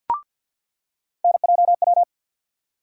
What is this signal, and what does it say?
Morse code, heard as a keyed tone, about 52 wpm, 700 Hz. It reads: NJW
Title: USA Call Sign Postfixes - 3 Letters 50wpm